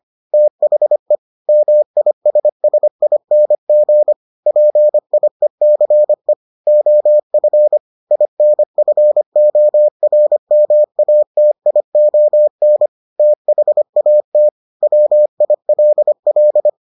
Morse code, heard as a keyed tone, about 25 wpm, 600 Hz. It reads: THE MISSING PIECE OF INFORMATION THAT WILL